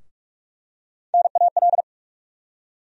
Morse code, heard as a keyed tone, about 46 wpm, 700 Hz. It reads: NAL
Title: USA Call Sign Postfixes - 3 Letters 45wpm